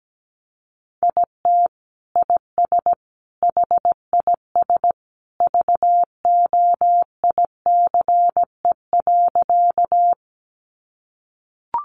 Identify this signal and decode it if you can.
Morse code, heard as a keyed tone, about 17 wpm, 700 Hz. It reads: IT IS HIS VOICE.
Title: Sentences from Top 500 Words 17wpm